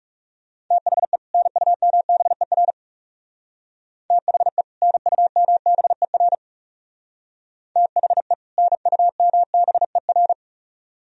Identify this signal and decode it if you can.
Morse code, heard as a keyed tone, about 37 wpm, 700 Hz. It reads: THE NUMBER THE NUMBER THE NUMBER
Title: Speed-Racing - Sets of 2 Words 30wpm